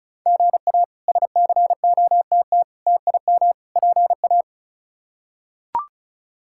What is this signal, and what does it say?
Morse code, heard as a keyed tone, about 36 wpm, 700 Hz. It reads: GA SCOTT TIM PA
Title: K1USN SST QSO Practice - 35wpm